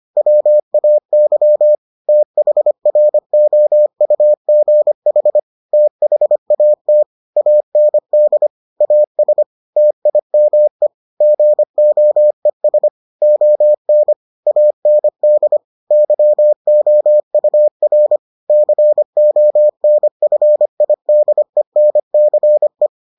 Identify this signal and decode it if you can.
Morse code, heard as a keyed tone, about 25 wpm, 600 Hz. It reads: WAY THROUGH THAT AND AS TIME GOES ON AND YOUR CONFIDENCE